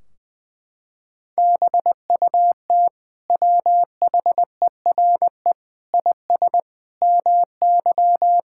Morse code, heard as a keyed tone, about 20 wpm, 700 Hz. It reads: BUT WHERE IS MY
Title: Sentences from Top 1600 Words 20wpm